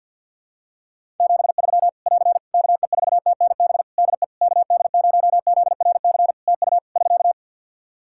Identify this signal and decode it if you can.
Morse code, heard as a keyed tone, about 49 wpm, 700 Hz. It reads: 73 <AR> KE4TNB DE KD0ZRC TU <SK>
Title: K1USN SST QSO Practice 50wpm